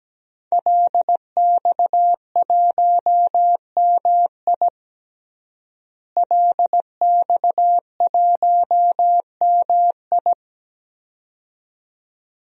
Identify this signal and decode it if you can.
Morse code, heard as a keyed tone, about 17 wpm, 700 Hz. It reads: LX1MI LX1MI
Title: DXCC-100 Call Signs - 5 Characters - Repeated 2x - 17wpm